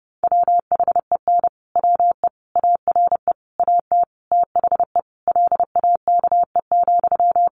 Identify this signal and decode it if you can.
Morse code, heard as a keyed tone, about 30 wpm, 700 Hz. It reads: WHEN WE ARE AT THE LAKE,
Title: Sentences from Top 1100 Words 30wpm